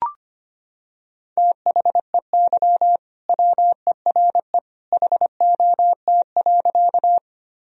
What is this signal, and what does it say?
Morse code, heard as a keyed tone, about 25 wpm, 700 Hz. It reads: THEY WERE HOT.
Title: Sentences from Top 1000 Words - Review Part 3 - 25wpm